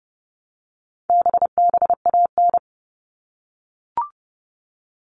Morse code, heard as a keyed tone, about 30 wpm, 700 Hz. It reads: BBAN
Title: All Cut Numbers - 4-Digit 30wpm